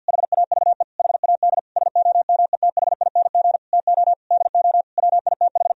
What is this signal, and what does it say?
Morse code, heard as a keyed tone, about 50 wpm, 700 Hz. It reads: HAVE HAD SOMETHING TO DO WITH
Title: Sentences from Top 1000 Words - Review Part 2 - 50wpm